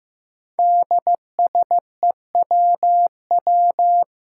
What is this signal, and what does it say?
Morse code, heard as a keyed tone, about 15 wpm, 700 Hz. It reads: DSEWW
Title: Five Challenging Letter-Number Combinations 15wpm